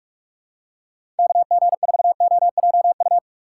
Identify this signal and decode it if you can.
Morse code, heard as a keyed tone, about 46 wpm, 690 Hz. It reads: KG4OJU
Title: USA Call Signs - 4x Character Spacing - 45wpm